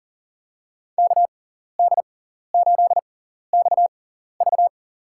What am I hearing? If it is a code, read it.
Morse code, heard as a keyed tone, about 37 wpm, spaced as for 14 wpm, 690 Hz. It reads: KD8XV
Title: USA Call Signs - 5 Characters - 6x Character Spacing - 40wpm